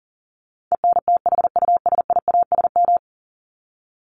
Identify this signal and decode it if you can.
Morse code, heard as a keyed tone, about 40 wpm, 700 Hz. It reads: ENTHUSIASM
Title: Top 1751-2000 Words 40wpm